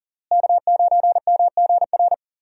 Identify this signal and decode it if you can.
Morse code, heard as a keyed tone, about 37 wpm, 700 Hz. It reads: K9MGR
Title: USA Call Signs - 5 Characters - Repeated 3x - 40wpm